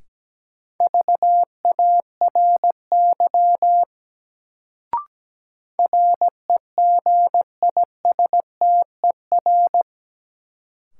Morse code, heard as a keyed tone, about 17 wpm, 700 Hz. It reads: VARY REGISTER
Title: Sentences from Top 1200 Words - New Word Drill Alt - 17wpm